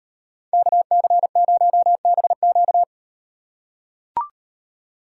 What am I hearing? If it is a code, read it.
Morse code, heard as a keyed tone, about 38 wpm, 700 Hz. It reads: KC0BQ